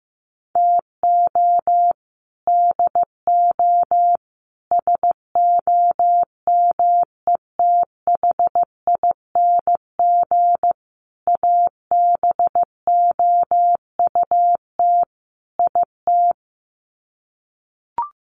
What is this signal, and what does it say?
Morse code, heard as a keyed tone, about 15 wpm, 700 Hz. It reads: TO DO SOMETHING ABOUT IT
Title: Sets of 5 Words 15wpm